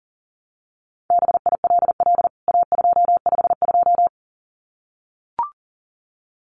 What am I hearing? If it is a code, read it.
Morse code, heard as a keyed tone, about 40 wpm, 700 Hz. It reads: BILL A252
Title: CWT Exchange 40wpm